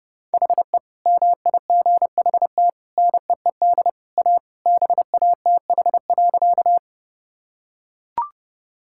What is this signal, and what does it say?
Morse code, heard as a keyed tone, about 32 wpm, 700 Hz. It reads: HE MIGHT NEED A BATH.
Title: Sentences from Top 1600 Words 30wpm